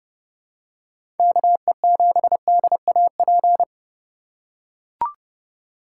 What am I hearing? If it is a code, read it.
Morse code, heard as a keyed tone, about 30 wpm, 700 Hz. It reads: KE7DAP
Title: USA Call Signs - 2x Character Spacing - 30wpm